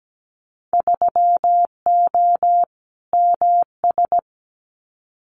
Morse code, heard as a keyed tone, about 17 wpm, 700 Hz. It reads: 3O MS
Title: Winter Field Day Exchanges Encore 17wpm